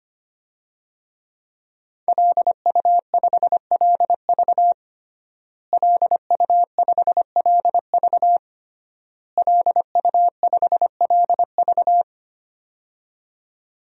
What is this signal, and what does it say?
Morse code, heard as a keyed tone, about 25 wpm, 700 Hz. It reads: LU5LV LU5LV LU5LV
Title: DXCC-100 Call Signs - 5 Characters - Repeated 3x - 25wpm